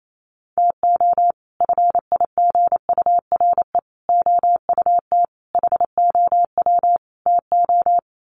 Morse code, heard as a keyed tone, about 28 wpm, 700 Hz. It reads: TO FIGURE OUT HOW TO
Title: Sets of 5 Words 28wpm